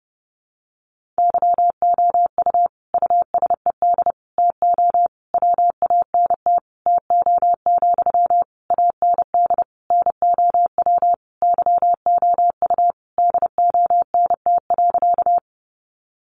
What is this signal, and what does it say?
Morse code, heard as a keyed tone, about 28 wpm, 700 Hz. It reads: YOU USED TO WANT TO, AND NOW YOU DONT.
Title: Sentences from Top 600 Words 30wpm